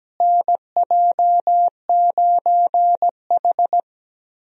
Morse code, heard as a keyed tone, about 17 wpm, 700 Hz. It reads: NJ9H
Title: USA Call Signs - 4 Characters - Repeated 3x - 17wpm